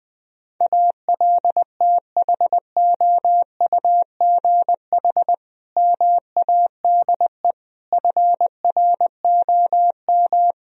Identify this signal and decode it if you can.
Morse code, heard as a keyed tone, about 20 wpm, 700 Hz. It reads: ALTHOUGH MADE FROM
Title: Sentences from Top 1000 Words - Review Part 3 - 20wpm